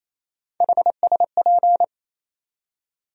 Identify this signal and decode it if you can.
Morse code, heard as a keyed tone, about 27 wpm, 700 Hz. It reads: HSP
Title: USA Call Sign Postfixes - 3 Letters 28wpm